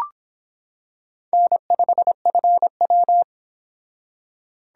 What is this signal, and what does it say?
Morse code, heard as a keyed tone, about 26 wpm, 700 Hz. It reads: N5FW